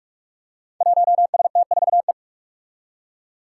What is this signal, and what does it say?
Morse code, heard as a keyed tone, about 44 wpm, 700 Hz. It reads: 1ST4E
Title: Five Challenging Letter-Number Combinations 45wpm